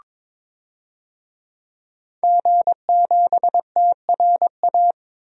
Morse code, heard as a keyed tone, about 22 wpm, 700 Hz. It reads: G7TRA
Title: DXCC-100 Call Signs - 5 Characters - 22wpm